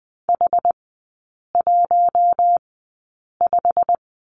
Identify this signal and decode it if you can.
Morse code, heard as a keyed tone, about 20 wpm, spaced as for 8 wpm, 700 Hz. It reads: H15
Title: 087 Lesson 15e - Number 5 - Three Characters 20wpm